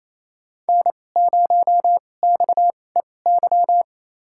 Morse code, heard as a keyed tone, about 28 wpm, spaced as for 20 wpm, 700 Hz. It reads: N0XEY
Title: USA Call Signs - 5 Characters - 2x Character Spacing - 28wpm